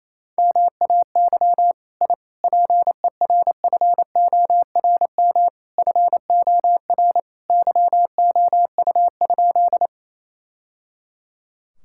Morse code, heard as a keyed tone, about 28 wpm, 700 Hz. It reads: MAY I PERFORM FOR YOU?
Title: Sentences from Top 1000 Words - Review Part 5 - 28wpm